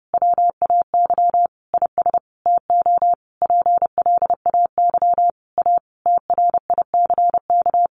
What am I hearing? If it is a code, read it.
Morse code, heard as a keyed tone, about 30 wpm, 700 Hz. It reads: WAY IS TO PLAY A TRICK
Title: Sentences from Top 1500 Words 30wpm